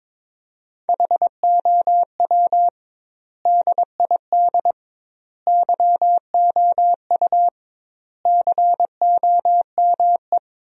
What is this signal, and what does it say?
Morse code, heard as a keyed tone, about 22 wpm, 700 Hz. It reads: HOW DID YOU COME
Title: 2x Word-Spacing - Sentences from Top 100 Words - 22wpm